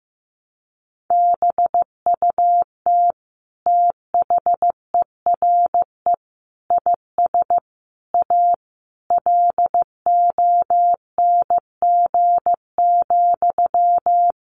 Morse code, heard as a keyed tone, about 15 wpm, 700 Hz. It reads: BUT THERE IS A LONG,
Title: Sentences from Top 1100 Words 15wpm